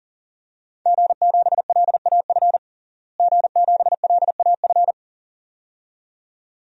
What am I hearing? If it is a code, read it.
Morse code, heard as a keyed tone, about 40 wpm, 700 Hz. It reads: G7LAF G7LAF
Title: DXCC-100 Call Signs - 5 Characters - Repeated 2x - 40wpm